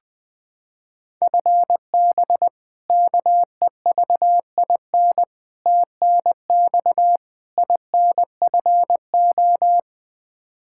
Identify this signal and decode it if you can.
Morse code, heard as a keyed tone, about 21 wpm, 700 Hz. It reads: FB KEVIN TNX INFO
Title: Practice QSO - Exchange 3 - More Info - 20wpm